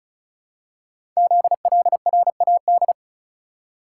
Morse code, heard as a keyed tone, about 36 wpm, 700 Hz. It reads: ZLRAD
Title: Code Groups - 5 Letter:Numbers - 35wpm